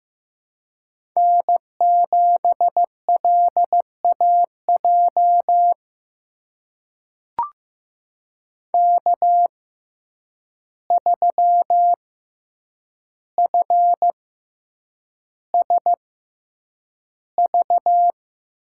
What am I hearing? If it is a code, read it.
Morse code, heard as a keyed tone, about 15 wpm, spaced as for 8 wpm, 700 Hz. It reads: N7LAJ K3FSV
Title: USA Call Signs - 5 Characters - 6x Character Spacing - 15wpm